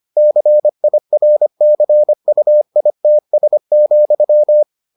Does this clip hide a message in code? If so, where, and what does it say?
Yes: Morse code from 0.2 to 4.6 s, heard as a keyed tone, about 25 wpm, 600 Hz. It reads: CIRCUITS,